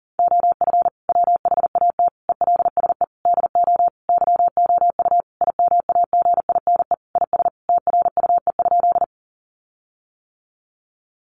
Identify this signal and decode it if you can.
Morse code, heard as a keyed tone, about 40 wpm, 700 Hz. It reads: OF WHAT ELSE DO YOU IMAGINE IS TRUE?
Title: Sentences from Top 1400 Words 40wpm